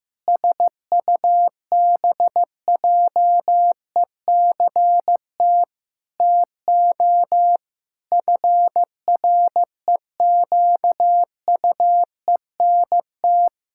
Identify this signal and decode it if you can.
Morse code, heard as a keyed tone, about 16 wpm, 700 Hz. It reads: SUBJECT TO FREQUENT
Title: Pangram Challenge v2 - 15wpm